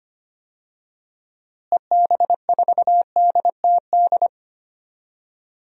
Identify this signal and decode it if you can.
Morse code, heard as a keyed tone, about 25 wpm, 700 Hz. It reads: EB4DTD
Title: DXCC-100 Call Signs - 6 Characters - Repeated 3x - 25wpm